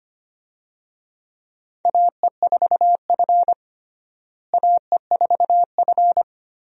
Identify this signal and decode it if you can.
Morse code, heard as a keyed tone, about 25 wpm, 700 Hz. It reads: AE4F AE4F
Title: USA Call Signs - 4 Characters - Repeated 3x - 25wpm